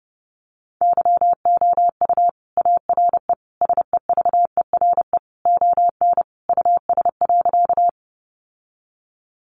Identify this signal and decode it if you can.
Morse code, heard as a keyed tone, about 30 wpm, 700 Hz. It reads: YOU ARE SEVERE ON US.
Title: Sentences from Top 1400 Words 30wpm